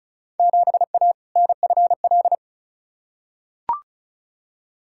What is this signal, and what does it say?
Morse code, heard as a keyed tone, about 36 wpm, 700 Hz. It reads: 7A NFL
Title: Field Day Exchanges Encore - 35wpm